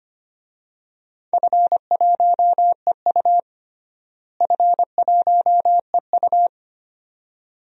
Morse code, heard as a keyed tone, about 25 wpm, 700 Hz. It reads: F1EU F1EU
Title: DXCC-100 Call Signs - 4 Characters - Repeated 2x - 25wpm